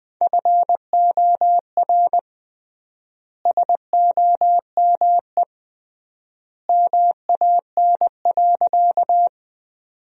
Morse code, heard as a keyed tone, about 21 wpm, 700 Hz. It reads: FOR SOME MAN.
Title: 3x Word-Spacing - Sentences from Top 200 Words - 20wpm